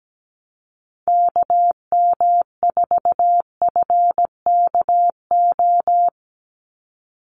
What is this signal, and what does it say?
Morse code, heard as a keyed tone, about 17 wpm, 700 Hz. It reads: KM4FKO
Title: USA Call Signs 17wpm